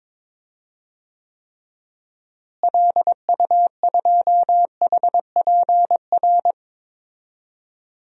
Morse code, heard as a keyed tone, about 22 wpm, 700 Hz. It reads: LU2HPR
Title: DXCC-100 Call Signs - 6 Characters - Repeated 2x - 22wpm